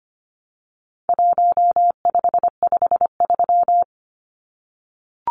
Morse code, heard as a keyed tone, about 25 wpm, 700 Hz. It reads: 1553
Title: Numbers - 4-Digit 25wpm